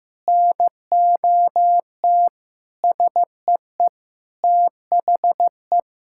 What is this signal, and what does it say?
Morse code, heard as a keyed tone, about 15 wpm, 700 Hz. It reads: NOT SEE THE
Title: Sentences from Top 1000 Words - Review Part 6 - 15wpm